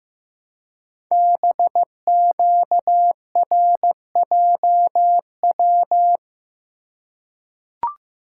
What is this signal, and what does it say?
Morse code, heard as a keyed tone, about 15 wpm, 700 Hz. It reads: BQRJW
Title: Code Groups - 5 Letters - 15wpm